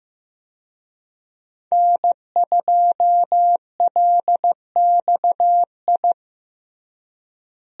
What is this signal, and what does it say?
Morse code, heard as a keyed tone, about 16 wpm, 700 Hz. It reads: N2LXI
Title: USA Call Signs - 5 Characters - Repeated 3x - 15wpm